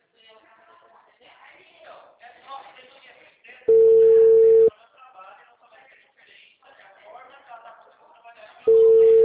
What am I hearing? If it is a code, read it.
Ringback tone (China, 450 Hz), with speech.